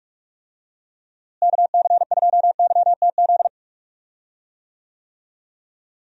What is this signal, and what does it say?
Morse code, heard as a keyed tone, about 42 wpm, 700 Hz. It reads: KC2YTZ
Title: USA Call Signs - 6 Characters - Repeated 2x - 45wpm